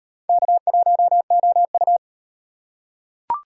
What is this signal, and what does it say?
Morse code, heard as a keyed tone, about 37 wpm, 700 Hz. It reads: K1OU